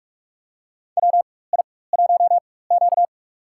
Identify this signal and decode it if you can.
Morse code, heard as a keyed tone, about 46 wpm, spaced as for 21 wpm, 700 Hz. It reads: WI1Q
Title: USA Call Signs - 4 Characters - 4x Character Spacing - 45wpm